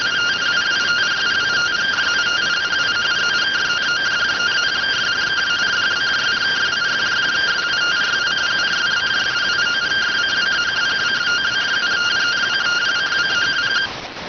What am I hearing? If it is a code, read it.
Two-tone teleprinter signal, about 170 Hz shift.